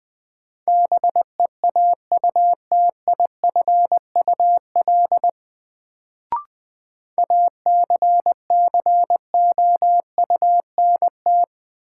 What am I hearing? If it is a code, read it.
Morse code, heard as a keyed tone, about 20 wpm, 700 Hz. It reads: BEAUTIFUL ACCOUNT
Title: Sentences from Top 900 Words - New Word Drill - 20wpm